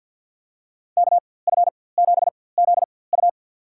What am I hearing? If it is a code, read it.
Morse code, heard as a keyed tone, about 51 wpm, spaced as for 23 wpm, 700 Hz. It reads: KF7ZU
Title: USA Call Signs - 4x Character Spacing - 50wpm